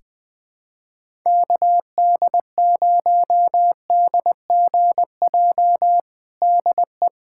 Morse code, heard as a keyed tone, about 20 wpm, 700 Hz. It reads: KD0DGJ DE
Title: USA Callsign DE Callsign - 6 Characters - 20wpm